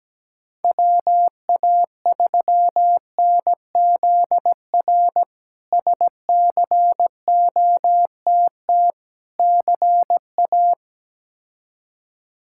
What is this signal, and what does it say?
Morse code, heard as a keyed tone, about 17 wpm, 700 Hz. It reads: WA3NZR SCOTT CA
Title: K1USN SST QSO Practice - 17wpm